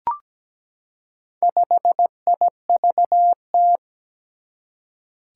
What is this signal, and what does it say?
Morse code, heard as a keyed tone, about 17 wpm, 700 Hz. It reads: 5IVT
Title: Four Challenging Letter-Number Combinations 17wpm